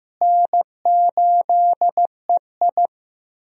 Morse code, heard as a keyed tone, about 15 wpm, 700 Hz. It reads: N8EI
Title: USA Call Signs - 4 Characters - Repeated 2x - 15wpm